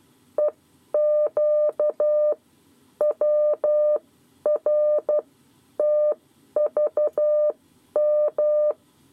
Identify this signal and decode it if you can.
Morse code, heard as a keyed tone, about 12 wpm, spaced as for 9 wpm, 600 Hz. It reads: EQWRTVM
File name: 7 Lezione cw 2